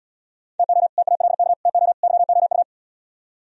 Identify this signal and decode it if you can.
Morse code, heard as a keyed tone, about 77 wpm, 700 Hz. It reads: TO TALK TO YOU